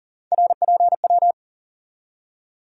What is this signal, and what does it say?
Morse code, heard as a keyed tone, about 40 wpm, 700 Hz. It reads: RPW
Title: USA Call Sign Postfixes - 3 Letters 40wpm